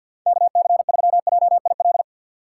Morse code, heard as a keyed tone, about 53 wpm, 700 Hz. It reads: KC3JIL
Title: USA Call Signs - 6x Character Spacing - 50wpm